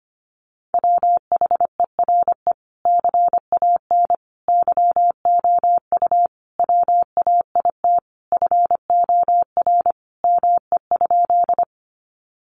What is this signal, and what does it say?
Morse code, heard as a keyed tone, about 25 wpm, 700 Hz. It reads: WHERE CAN YOU WAIT FOR ME?
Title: Sentences from Top 500 Words 25wpm